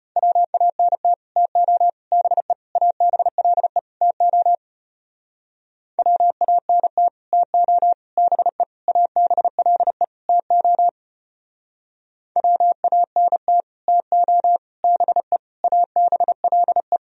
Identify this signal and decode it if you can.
Morse code, heard as a keyed tone, about 30 wpm, 700 Hz. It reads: WANT TO BE ABLE TO WANT TO BE ABLE TO WANT TO BE ABLE
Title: Speed-Racing - Sets of 5 Words 25wpm